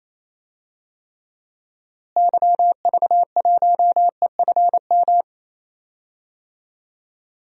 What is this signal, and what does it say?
Morse code, heard as a keyed tone, about 27 wpm, 700 Hz. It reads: YV1EFM
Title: DXCC-100 Call Signs - 6 Characters - Repeated 3x - 28wpm